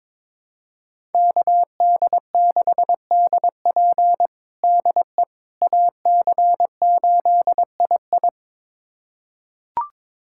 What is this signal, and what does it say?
Morse code, heard as a keyed tone, about 21 wpm, 700 Hz. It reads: KD6DP DE AC8II
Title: USA Callsign DE Callsign - 5 Characters - 22wpm